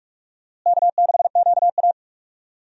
Morse code, heard as a keyed tone, about 48 wpm, 700 Hz. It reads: K6QA
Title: USA Call Signs - 4 Characters - Repeated 2x - 45wpm